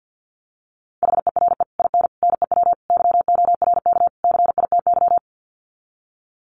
Morse code, heard as a keyed tone, about 50 wpm, 700 Hz. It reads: HERE IN NEW YORK CITY
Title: Sets of 5 Words Encore2 50wpm